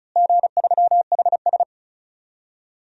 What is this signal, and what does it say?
Morse code, heard as a keyed tone, about 36 wpm, 700 Hz. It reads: G3HS
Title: DXCC-100 Call Signs - 4 Characters - Repeated 3x - 35wpm